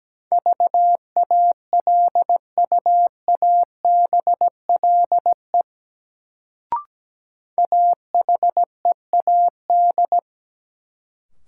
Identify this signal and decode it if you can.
Morse code, heard as a keyed tone, about 17 wpm, 700 Hz. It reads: VALUABLE AHEAD
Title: Sentences from Top 1100 Words - New Word Drill Alt - 17wpm